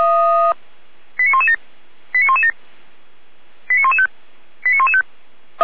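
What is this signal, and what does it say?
Five-tone selective call (ZVEI), heard as quick runs of five tones, tones of about 80 ms. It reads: 89117 89117 89115 89115
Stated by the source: Received on SDR, ZVEI five-tone call 89080.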